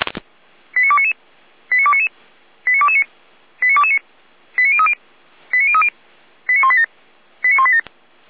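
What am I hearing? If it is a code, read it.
Five-tone selective call (ZVEI), heard as quick runs of five tones, tones of about 70 ms. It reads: 89299 89299 89229 89229 89030 89030 89187 89187
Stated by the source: ZVEI five-tone selective call received by SDR, feuerwehr-orlamuende.org (Orlamünde fire brigade) server.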